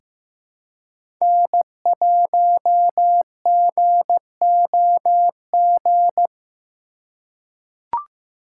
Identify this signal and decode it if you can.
Morse code, heard as a keyed tone, about 15 wpm, 700 Hz. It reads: N1GOG
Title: USA Call Signs - 5 Characters - 4x Character Spacing - 15wpm